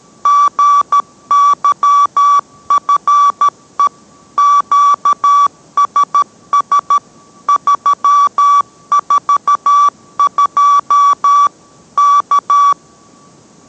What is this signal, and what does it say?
Morse code, heard as a keyed tone, about 13 wpm, 1150 Hz. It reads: GYFE QSS 342 K